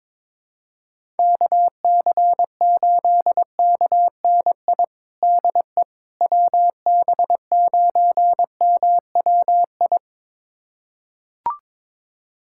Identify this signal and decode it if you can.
Morse code, heard as a keyed tone, about 22 wpm, 700 Hz. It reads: KC8KNI DE WB9MWI
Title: USA Callsign DE Callsign 22wpm